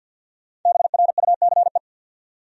Morse code, heard as a keyed tone, about 52 wpm, 700 Hz. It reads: BRUCE
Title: USA Common Names - 50wpm